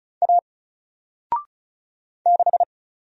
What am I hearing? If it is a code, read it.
Morse code, heard as a keyed tone, about 35 wpm, 700 Hz. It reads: A 6
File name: Single-Letter-Number-35wpm